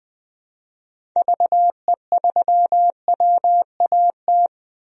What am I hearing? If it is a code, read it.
Morse code, heard as a keyed tone, about 21 wpm, 700 Hz. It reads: VE3WAT
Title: DXCC-100 Call Signs - 6 Characters - 20wpm